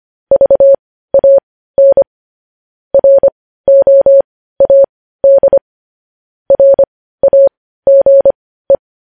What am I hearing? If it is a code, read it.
Morse code, heard as a keyed tone, about 25 wpm, spaced as for 15 wpm, 550 Hz. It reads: VAN ROAD RAGE
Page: News Headlines in Morse Code at 15 WPM